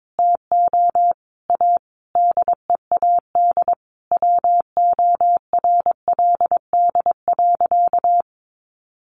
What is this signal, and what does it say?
Morse code, heard as a keyed tone, about 22 wpm, 700 Hz. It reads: TO A DEAD WORLD.
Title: Sentences from Top 1500 Words 22wpm